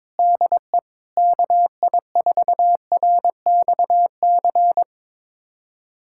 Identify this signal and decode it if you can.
Morse code, heard as a keyed tone, about 22 wpm, 700 Hz. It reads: DE KI4RXC
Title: USA Callsign DE Callsign 22wpm